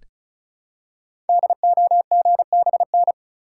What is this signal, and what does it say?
Morse code, heard as a keyed tone, about 36 wpm, 690 Hz. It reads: DOGBN
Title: Code Groups - 5 Letters - 35wpm